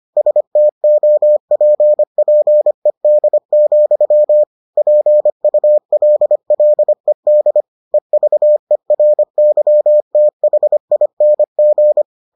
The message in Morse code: STOPPED, PULLED EVERYTHING

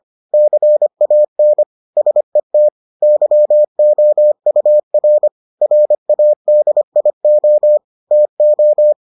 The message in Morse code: CAN SET YOUR RADIO TO

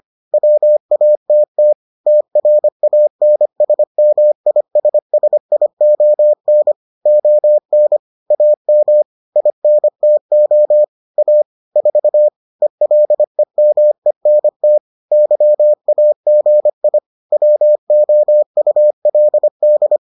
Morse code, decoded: WATT TRANSMISSION ON AM INTO A 4 ELEMENT YAGI WOULD

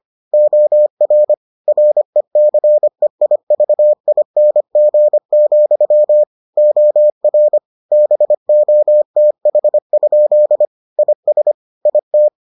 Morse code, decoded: OR RECEIVING, OR BOTH? IS IT

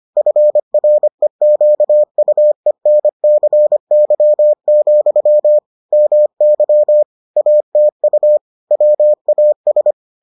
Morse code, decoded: FREQUENCY, MY ATU WAS

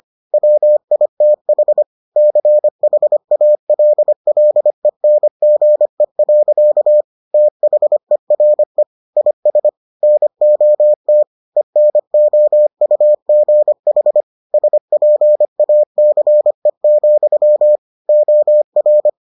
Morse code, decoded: WITH CHALLENGE. THERE IS NOT ENOUGH SPACE, OR

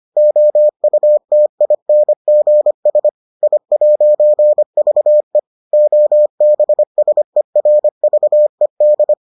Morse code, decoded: OUTINGS I'VE OBSERVED

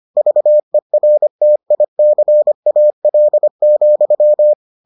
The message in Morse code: VERTICAL,

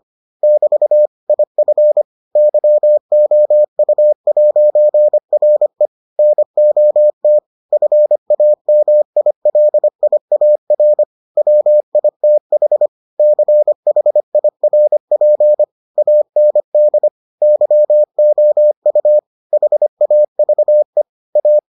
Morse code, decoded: <BT> IF YOU'RE NOT FAMILIAR WITH CHIRP AND YOU HAVE A

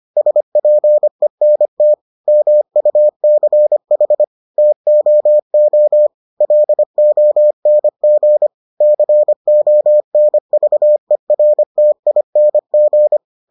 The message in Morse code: SPENT MUCH TOO LONG CONVERTING

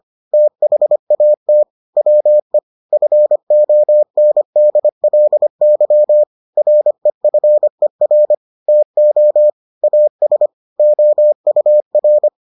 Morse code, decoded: THAT WE FONDLY REFER TO AS OUR

